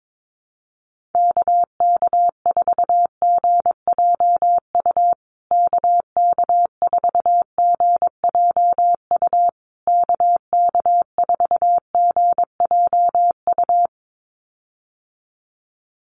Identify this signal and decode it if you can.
Morse code, heard as a keyed tone, about 22 wpm, 700 Hz. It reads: KK4GJU KK4GJU KK4GJU
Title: USA Call Signs Repeated 3x 22wpm